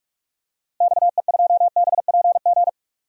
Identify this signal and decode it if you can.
Morse code, heard as a keyed tone, about 46 wpm, 700 Hz. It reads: XE2BPG
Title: DXCC-100 Call Signs - 6 Characters - Repeated 3x - 45wpm